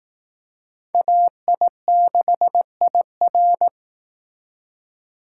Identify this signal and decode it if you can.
Morse code, heard as a keyed tone, about 18 wpm, 700 Hz. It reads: AI6IR
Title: Speed Racing - USA 2x2 Call Signs 15wpm